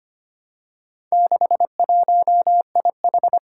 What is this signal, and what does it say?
Morse code, heard as a keyed tone, about 25 wpm, 700 Hz. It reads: 61IH
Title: Four Challenging Letter-Number Combinations 25wpm